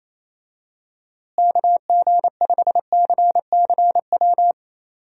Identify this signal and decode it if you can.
Morse code, heard as a keyed tone, about 28 wpm, 700 Hz. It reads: KG5CCW
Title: USA Call Signs - 6 Characters - Repeated 3x - 28wpm